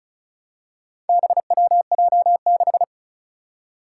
Morse code, heard as a keyed tone, about 36 wpm, 700 Hz. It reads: BWJ6
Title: Four Challenging Letter-Number Combinations 35wpm